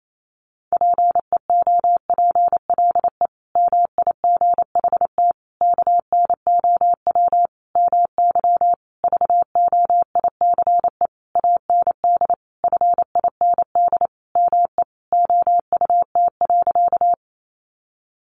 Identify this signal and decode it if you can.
Morse code, heard as a keyed tone, about 28 wpm, 700 Hz. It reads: PEOPLE MIGHT KNOW MY VOICE AND FIND ME OUT.
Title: Sentences from Top 500 Words 28wpm